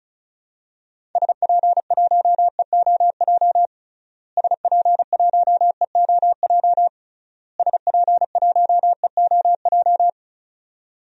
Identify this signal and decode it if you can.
Morse code, heard as a keyed tone, about 36 wpm, 700 Hz. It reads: SP1EOJ SP1EOJ SP1EOJ
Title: DXCC-100 Call Signs - 6 Characters - Repeated 3x - 35wpm